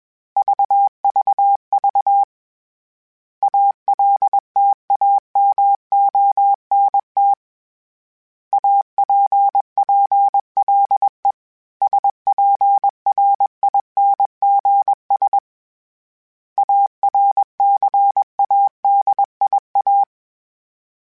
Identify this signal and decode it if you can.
Morse code, heard as a keyed tone, about 21 wpm, 800 Hz. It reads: VVV ALTAMONT APPLE SPRINGS ARCADIA
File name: fromSMs / FromSMs mp3s / Cities@20